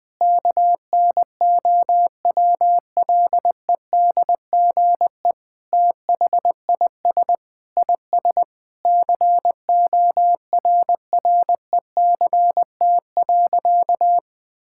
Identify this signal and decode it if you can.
Morse code, heard as a keyed tone, about 20 wpm, 700 Hz. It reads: KNOWLEDGE THIS IS CORRECT.
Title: Sentences with 8-Word Phrases - Spoken Beginning - 20wpm